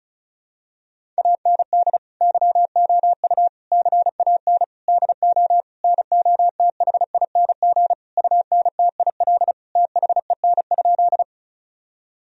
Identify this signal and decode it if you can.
Morse code, heard as a keyed tone, about 36 wpm, 700 Hz. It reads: AND YOU CAN DO NOTHING UNTIL THEN?
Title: Sentences from Top 1000 Words - Review Part 1 - 35wpm